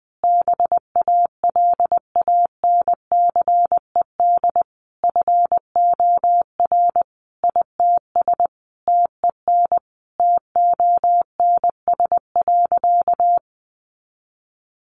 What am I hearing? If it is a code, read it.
Morse code, heard as a keyed tone, about 19 wpm, 700 Hz. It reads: BALANCED FOR ITS TEN TONS.
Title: A Sound Of Thunder 20wpm